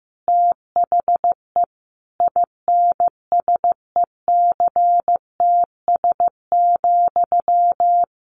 Morse code, heard as a keyed tone, about 15 wpm, 700 Hz. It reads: THE INSECTS,
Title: A Sound Of Thunder 15wpm